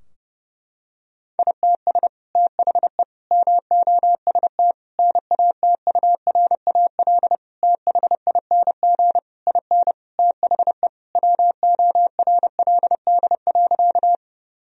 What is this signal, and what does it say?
Morse code, heard as a keyed tone, about 32 wpm, 700 Hz. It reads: ITS THE MOST NATURAL THING IN THE WORLD.
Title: Sentences from Top 1000 Words - Review Part 4 - 30wpm